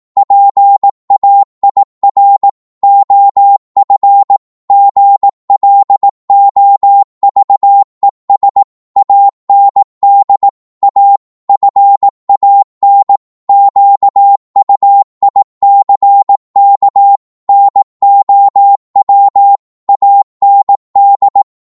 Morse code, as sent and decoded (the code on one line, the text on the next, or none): .--. .- .. .-. / --- ..-. / --. .-.. --- ...- . ... / .- -. -.. / .- / ..-. .- -. / --.- ..- .. -.-. -.- / -. --- .-- / .- -. -..
PAIR OF GLOVES AND A FAN QUICK NOW AND